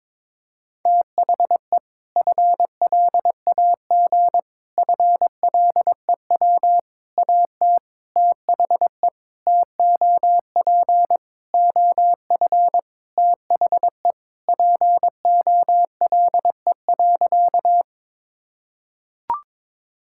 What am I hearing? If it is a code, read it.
Morse code, heard as a keyed tone, about 22 wpm, 700 Hz. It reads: THE FLAG FLEW AT THE TOP OF THE POLE.
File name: practice / 22wpm / Sentences with 4-Word Phrases - 22wpm